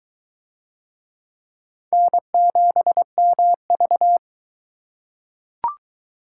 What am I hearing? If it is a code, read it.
Morse code, heard as a keyed tone, about 21 wpm, 700 Hz. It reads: N7MV